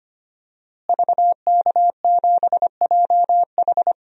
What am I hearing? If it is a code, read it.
Morse code, heard as a keyed tone, about 25 wpm, 700 Hz. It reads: VK7JH
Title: DXCC-100 Call Signs - 5 Characters - Repeated 2x - 25wpm